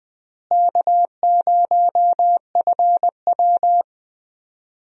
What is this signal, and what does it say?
Morse code, heard as a keyed tone, about 21 wpm, 700 Hz. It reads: K0FW
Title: USA Call Signs - 4 Characters - Repeated 2x - 20wpm